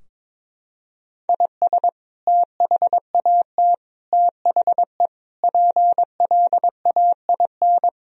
Morse code, heard as a keyed tone, about 22 wpm, 700 Hz. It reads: IS THAT THE PLAIN
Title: Sentences from Top 1000 Words - Review Part 2 - 22wpm